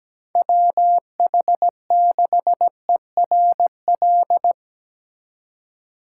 Morse code, heard as a keyed tone, about 19 wpm, 700 Hz. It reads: WH6ERL
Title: Speed Racing - USA 2x3 Call Signs 17wpm